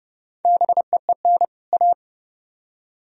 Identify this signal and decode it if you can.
Morse code, heard as a keyed tone, about 39 wpm, 700 Hz. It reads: BEEN A